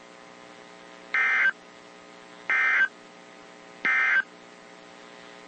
Emergency Alert System (SAME) tones, heard as harsh buzzing bursts. It is heard from 1.0 to 4.3 s.
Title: WZ2523 - NOAA Weather Radio